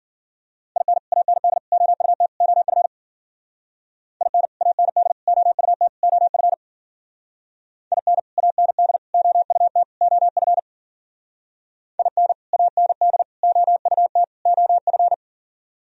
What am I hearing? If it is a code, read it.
Morse code, heard as a keyed tone, about 51 wpm, 700 Hz. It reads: IN AND OUT OF IN AND OUT OF IN AND OUT OF IN AND OUT OF